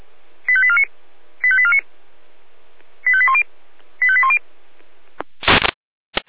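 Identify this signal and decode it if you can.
Five-tone selective call (ZVEI), heard as quick runs of five tones, tones of about 70 ms. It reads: 85849 85849 85810 85810
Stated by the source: ZVEI five-tone selective call received with an SDR.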